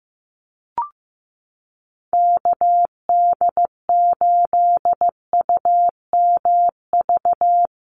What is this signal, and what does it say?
Morse code, heard as a keyed tone, about 15 wpm, 700 Hz. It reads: KD8UMV
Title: USA Call Signs Repeated 2x 15wpm